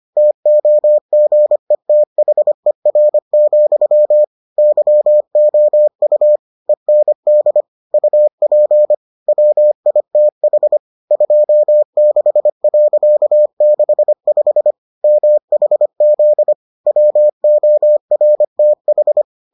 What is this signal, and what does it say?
Morse code, heard as a keyed tone, about 25 wpm, 600 Hz. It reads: TOGETHER, YOU END UP WITH 26.65 MHZ WORTH